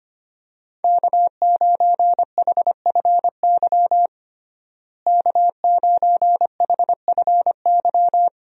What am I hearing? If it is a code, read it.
Morse code, heard as a keyed tone, about 25 wpm, 690 Hz. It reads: K9HFY K9HFY
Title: USA Call Signs - 5 Characters - Repeated 2x - 25wpm